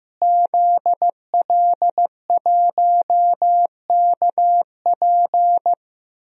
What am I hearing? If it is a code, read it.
Morse code, heard as a keyed tone, about 15 wpm, 700 Hz. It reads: ZL1KP
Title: DXCC-100 Call Signs - 5 Characters - Repeated 3x - 15wpm